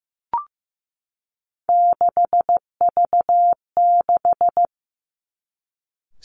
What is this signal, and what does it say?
Morse code, heard as a keyed tone, about 15 wpm, 700 Hz. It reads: 6V6
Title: Three Challenging Letter-Number Combinations 15wpm